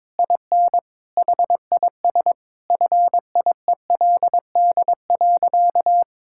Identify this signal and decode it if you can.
Morse code, heard as a keyed tone, about 22 wpm, 700 Hz. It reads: IN HIS FIELD.
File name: Sentences with 7-Word Phrases - 22wpm